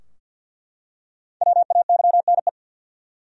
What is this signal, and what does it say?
Morse code, heard as a keyed tone, about 47 wpm, 700 Hz. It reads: WAYNE
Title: USA Common Names - 50wpm